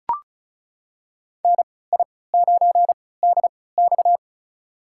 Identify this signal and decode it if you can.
Morse code, heard as a keyed tone, about 34 wpm, spaced as for 20 wpm, 700 Hz. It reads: NI9DX
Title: USA Call Signs - 5 Characters - 3x Character Spacing - 35wpm